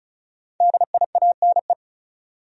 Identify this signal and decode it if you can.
Morse code, heard as a keyed tone, about 36 wpm, 700 Hz. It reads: DIANE
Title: USA Common Names - 35wpm